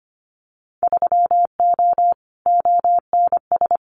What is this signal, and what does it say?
Morse code, heard as a keyed tone, about 25 wpm, 700 Hz. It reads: 3O ONS
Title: Winter Field Day Exchanges Encore 25wpm